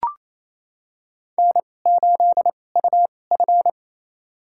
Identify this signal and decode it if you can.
Morse code, heard as a keyed tone, about 27 wpm, spaced as for 20 wpm, 700 Hz. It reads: N8UF
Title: USA Call Signs - 4 Characters - 2x Character Spacing - 28wpm